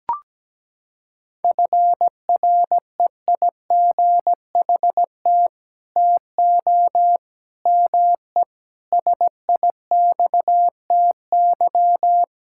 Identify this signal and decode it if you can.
Morse code, heard as a keyed tone, about 17 wpm, 700 Hz. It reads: FREIGHT TO ME SIXTY
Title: Pangram Challenge v2 - 17wpm